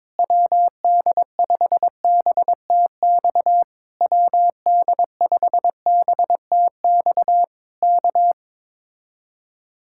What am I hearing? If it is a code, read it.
Morse code, heard as a keyed tone, about 21 wpm, 700 Hz. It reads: WD5BTX WD5BTX K
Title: Speed-Racing Common QSO Phrases Encore - 22wpm